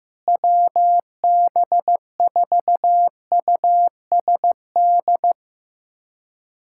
Morse code, heard as a keyed tone, about 15 wpm, 700 Hz. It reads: WB4USD
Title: USA Callsign DE Callsign - 6 Characters - 15wpm